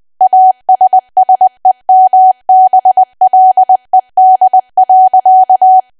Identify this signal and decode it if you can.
Morse code, heard as a keyed tone, about 20 wpm, 750 Hz. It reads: ASSEMBLED.
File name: cwa / W1AW-Level3 / 20WPM